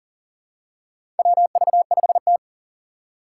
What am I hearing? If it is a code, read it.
Morse code, heard as a keyed tone, about 37 wpm, 700 Hz. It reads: WV5T
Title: Four Challenging Letter-Number Combinations 40wpm